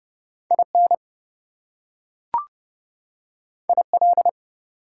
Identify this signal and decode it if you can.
Morse code, heard as a keyed tone, about 31 wpm, 700 Hz. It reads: IN IL
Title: US States and Canadian Provinces 30wpm